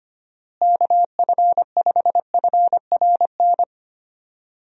Morse code, heard as a keyed tone, about 25 wpm, 700 Hz. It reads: KF5FRN
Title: USA Call Signs - 6 Characters - Repeated 2x - 25wpm